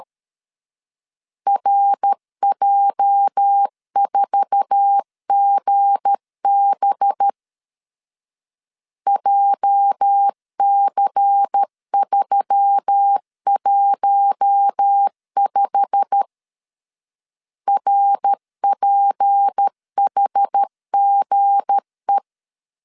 Morse code, heard as a keyed tone, about 13 wpm, 800 Hz. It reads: RJ4GB JC315 RPHGE
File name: CW / DET / 1 9 1500C S B 12wpm 3819